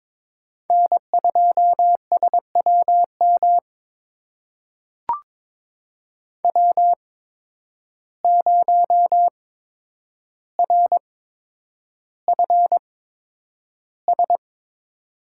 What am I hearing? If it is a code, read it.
Morse code, heard as a keyed tone, about 21 wpm, spaced as for 9 wpm, 700 Hz. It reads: N2SWM W0RFS
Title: USA Call Signs - 5 Characters - 8x Character Spacing - 22wpm